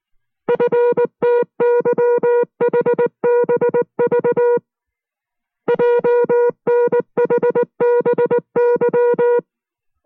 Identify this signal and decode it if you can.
Morse code, heard as a keyed tone, about 19 wpm, 480 Hz. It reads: FTYHBV JNHBY